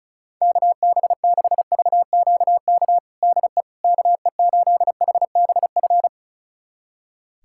Morse code, heard as a keyed tone, about 34 wpm, 700 Hz. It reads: KB6VQK DE KE8HBF
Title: USA Callsign DE Callsign - 6 Characters - 35wpm